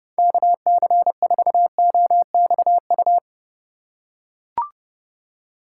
Morse code, heard as a keyed tone, about 30 wpm, 700 Hz. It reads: KC4OXU
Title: USA Call Signs - 2x Character Spacing - 30wpm